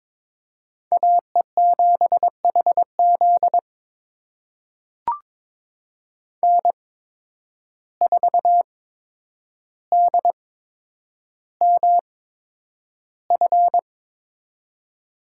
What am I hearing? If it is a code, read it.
Morse code, heard as a keyed tone, about 22 wpm, spaced as for 9 wpm, 700 Hz. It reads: AE7HZ N4DMF
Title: USA Call Signs - 5 Characters - 8x Character Spacing - 22wpm